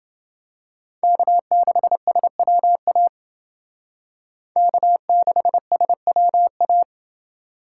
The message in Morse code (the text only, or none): K6SWA K6SWA